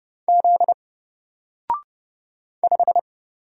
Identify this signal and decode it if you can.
Morse code, heard as a keyed tone, about 30 wpm, 700 Hz. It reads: Z 5
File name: Single-Letter-Number-30wpm